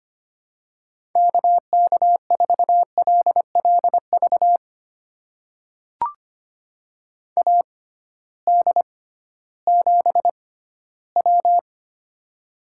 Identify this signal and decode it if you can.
Morse code, heard as a keyed tone, about 25 wpm, spaced as for 15 wpm, 700 Hz. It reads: KK4LLV AD7W
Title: USA Call Signs - 6x Character Spacing - 25wpm